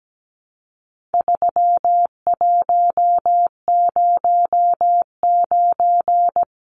Morse code, heard as a keyed tone, about 17 wpm, 700 Hz. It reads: 3109
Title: Numbers - 4-Digit 17wpm